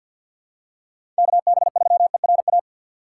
Morse code, heard as a keyed tone, about 51 wpm, 700 Hz. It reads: KB3ERA
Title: USA Call Signs - 6 Characters - 4x Character Spacing - 50wpm